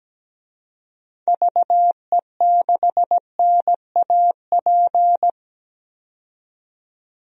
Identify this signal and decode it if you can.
Morse code, heard as a keyed tone, about 17 wpm, 700 Hz. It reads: VE6NAP
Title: DXCC-100 Call Signs - 6 Characters - 17wpm